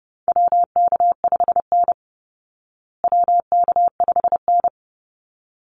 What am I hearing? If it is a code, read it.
Morse code, heard as a keyed tone, about 30 wpm, 700 Hz. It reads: WK5N WK5N
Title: Call Signs Repeated 3x - 30wpm